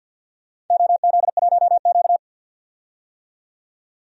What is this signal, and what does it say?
Morse code, heard as a keyed tone, about 51 wpm, 700 Hz. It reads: OZ1Q
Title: DXCC-100 Call Signs - 4 Characters - 50wpm